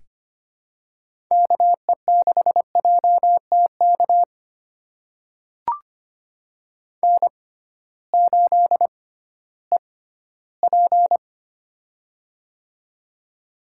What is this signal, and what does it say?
Morse code, heard as a keyed tone, about 25 wpm, spaced as for 15 wpm, 700 Hz. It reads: KE6JTK N8EP
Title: USA Call Signs - 6x Character Spacing - 25wpm